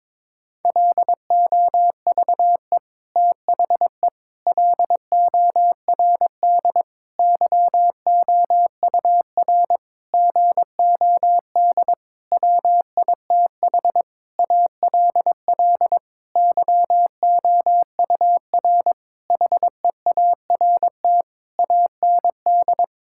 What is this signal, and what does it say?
Morse code, heard as a keyed tone, about 22 wpm, 700 Hz. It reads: LOVE THE LORD YOUR GOD WITH ALL YOUR HEART AND
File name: Sentences with 9-Word Phrases - 22wpm